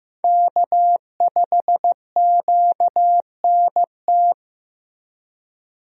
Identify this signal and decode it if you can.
Morse code, heard as a keyed tone, about 15 wpm, 700 Hz. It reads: K5QNT
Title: USA Call Signs - 5 Characters - Repeated 2x - 15wpm